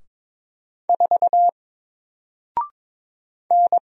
Morse code, heard as a keyed tone, about 22 wpm, 700 Hz. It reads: 4 N
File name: Single-Letter-Number-22wpm